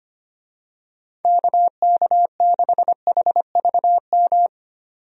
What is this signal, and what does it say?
Morse code, heard as a keyed tone, about 25 wpm, 700 Hz. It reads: KK6HVM
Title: USA Call Signs - 6 Characters - 25wpm